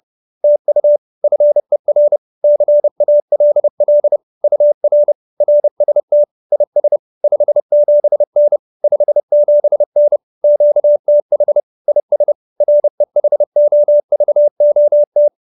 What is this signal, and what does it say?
Morse code, heard as a keyed tone, about 30 wpm, 590 Hz. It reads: TU FER CALL UR RST IS 57N 57N QTH IS REHOVOT
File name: QSO310 30